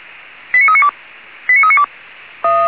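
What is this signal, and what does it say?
Five-tone selective call (ZVEI), heard as quick runs of five tones, tones of about 70 ms. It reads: 89392 89392
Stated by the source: ZVEI five-tone selective call received with an SDR.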